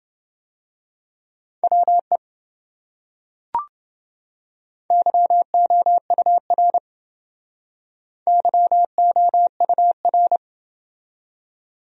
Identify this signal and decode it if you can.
Morse code, heard as a keyed tone, about 28 wpm, 700 Hz. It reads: WE YOUR YOUR